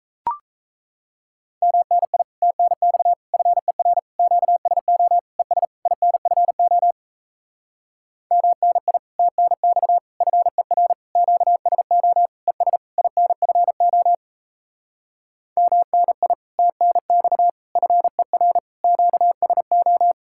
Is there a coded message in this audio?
Yes: Morse code from 1.6 to 20.1 s, heard as a keyed tone, about 37 wpm, 700 Hz. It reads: MNI TNX FER QSO ES INFO MNI TNX FER QSO ES INFO MNI TNX FER QSO